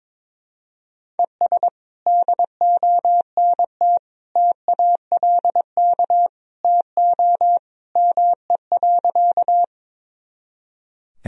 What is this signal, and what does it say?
Morse code, heard as a keyed tone, about 22 wpm, 700 Hz. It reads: ES DONT TALK TO ME.
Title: Sentences from Top 400 Words and Abbreviations 22wpm